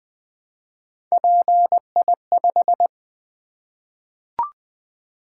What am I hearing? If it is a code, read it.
Morse code, heard as a keyed tone, about 20 wpm, 700 Hz. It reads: PI5
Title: Three Letter-Number Combinations - 20wpm